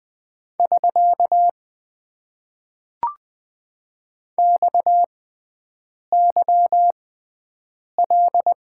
Morse code, heard as a keyed tone, about 20 wpm, spaced as for 7 wpm, 700 Hz. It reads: <SK> XYL
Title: Top 90 QSO Elements - 6x Character Spacing - 20wpm